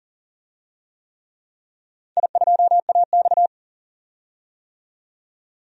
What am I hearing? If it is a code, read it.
Morse code, heard as a keyed tone, about 39 wpm, 700 Hz. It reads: I2AX
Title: DXCC-100 Call Signs - 4 Characters - 40wpm